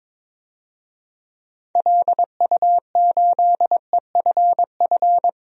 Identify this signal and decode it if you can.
Morse code, heard as a keyed tone, about 22 wpm, 700 Hz. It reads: LU8EFF
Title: DXCC-100 Call Signs - 6 Characters - 22wpm